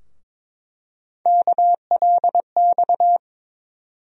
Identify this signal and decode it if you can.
Morse code, heard as a keyed tone, about 22 wpm, 700 Hz. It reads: KLX
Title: USA Call Sign Postfixes - 3 Letters 22wpm